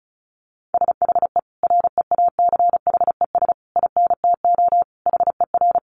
Morse code, heard as a keyed tone, about 34 wpm, 700 Hz. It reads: SHE REACHES INTO HER
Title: Additional Easy Phrases 35wpm